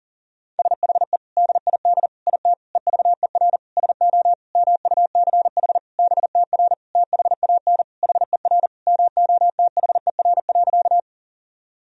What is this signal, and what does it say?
Morse code, heard as a keyed tone, about 40 wpm, 700 Hz. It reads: SHE DID IT EVER SO MUCH BTR THAN HER MOTHER.
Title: Sentences from Top 400 Words and Abbreviations 40wpm